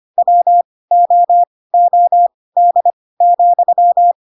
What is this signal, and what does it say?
Morse code, heard as a keyed tone, about 25 wpm, spaced as for 18 wpm, 700 Hz. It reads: WOOD,
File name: Pinocchio / Pinocchio ch2 12 25 ik1hge